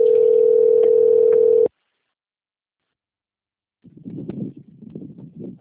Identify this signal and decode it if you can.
Ringback tone (North America), 440 + 480 Hz.